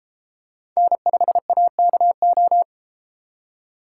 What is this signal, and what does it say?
Morse code, heard as a keyed tone, about 32 wpm, 700 Hz. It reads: N5AKO